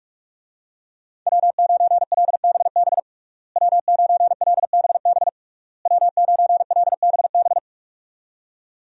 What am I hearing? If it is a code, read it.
Morse code, heard as a keyed tone, about 46 wpm, 700 Hz. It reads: W9LBB W9LBB W9LBB
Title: USA Call Signs - 5 Characters - Repeated 3x - 45wpm